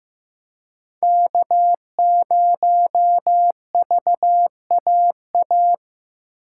Morse code, heard as a keyed tone, about 18 wpm, 700 Hz. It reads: K0VAA
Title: Speed Racing - USA 1x3 Call Signs 15wpm